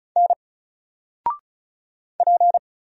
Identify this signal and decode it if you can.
Morse code, heard as a keyed tone, about 33 wpm, 700 Hz. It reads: N P
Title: Single Character (Letter:Number:Punctuation:Prosign) 35wpm